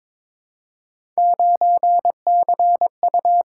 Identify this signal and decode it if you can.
Morse code, heard as a keyed tone, about 22 wpm, 700 Hz. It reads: 9CU
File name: practice / 22wpm / Three Letter-Number Combinations - 22wpm